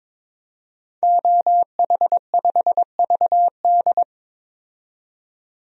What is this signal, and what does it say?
Morse code, heard as a keyed tone, about 22 wpm, 700 Hz. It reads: OH5VD
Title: DXCC-100 Call Signs - 5 Characters - 22wpm